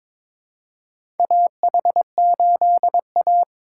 Morse code, heard as a keyed tone, about 23 wpm, 700 Hz. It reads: AH8A
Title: USA Call Signs - 4 Characters - 2x Character Spacing - 22wpm